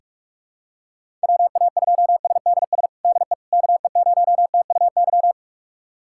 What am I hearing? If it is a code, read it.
Morse code, heard as a keyed tone, about 46 wpm, 700 Hz. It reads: WA2SDS DE KE0TUY
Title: USA Callsign DE Callsign - 6 Characters - 45wpm